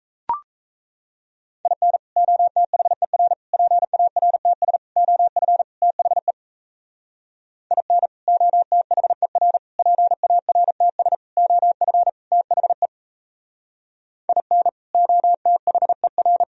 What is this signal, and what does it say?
Morse code, heard as a keyed tone, about 38 wpm, 700 Hz. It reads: IN OTHER PARTS OF THE IN OTHER PARTS OF THE IN OTHER